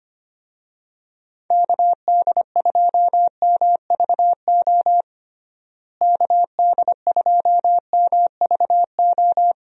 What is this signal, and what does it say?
Morse code, heard as a keyed tone, about 25 wpm, 700 Hz. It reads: KD2MVO KD2MVO
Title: USA Call Signs - 6 Characters - Repeated 2x - 25wpm